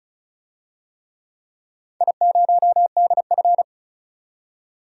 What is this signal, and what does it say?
Morse code, heard as a keyed tone, about 36 wpm, 700 Hz. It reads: I0DF
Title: DXCC-100 Call Signs - 4 Characters - 35wpm